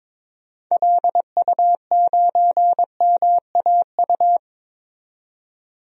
Morse code, heard as a keyed tone, about 21 wpm, 700 Hz. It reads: LU9MAU
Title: DXCC-100 Call Signs - 6 Characters - 22wpm